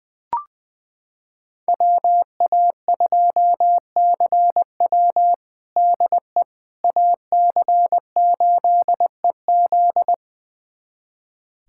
Morse code, heard as a keyed tone, about 20 wpm, 700 Hz. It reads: WA2CW DE AC8EZ
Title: USA Callsign DE Callsign - 5 Characters - 20wpm